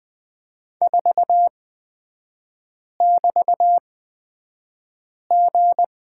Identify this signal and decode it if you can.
Morse code, heard as a keyed tone, about 20 wpm, spaced as for 5 wpm, 700 Hz. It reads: 4<BT>G
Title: 224 Lesson 36e - BT - Three Characters 20wpm